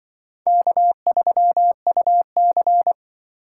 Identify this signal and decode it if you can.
Morse code, heard as a keyed tone, about 24 wpm, 700 Hz. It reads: K3UC
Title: Speed Racing - USA 1x2 Call Signs 20wpm